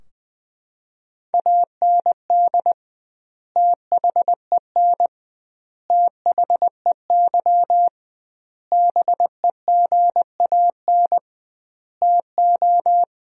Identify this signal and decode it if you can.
Morse code, heard as a keyed tone, about 20 wpm, 700 Hz. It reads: AND THEN THEY BEGAN TO
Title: 2x Word-Spacing - Sentences from Top 300 Words - 20wpm